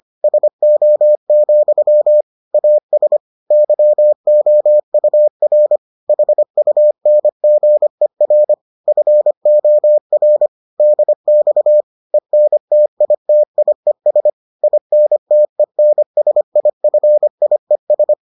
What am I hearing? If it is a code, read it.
Morse code, heard as a keyed tone, about 25 wpm, 600 Hz. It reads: SO, AS YOUR HUNGER FOR DX ENTITIES INTENSIFIES